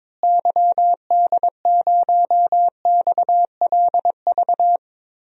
Morse code, heard as a keyed tone, about 21 wpm, 700 Hz. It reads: YD0XLV
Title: DXCC-100 Call Signs - 6 Characters - Repeated 3x - 22wpm